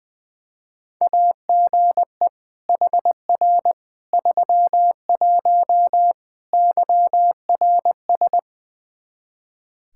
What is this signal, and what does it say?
Morse code, heard as a keyed tone, about 20 wpm, 700 Hz. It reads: AGE HR 31 YRS
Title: Practice QSO - Exchange 3 - More Info - 20wpm